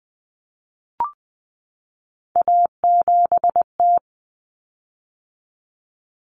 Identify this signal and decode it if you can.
Morse code, heard as a keyed tone, about 20 wpm, 700 Hz. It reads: A7T
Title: Common Cut Numbers - 3-Digit 20wpm